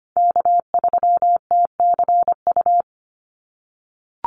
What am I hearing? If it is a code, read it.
Morse code, heard as a keyed tone, about 25 wpm, 700 Hz. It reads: K3TCU
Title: NA SOTA Chaser Callsigns 25wpm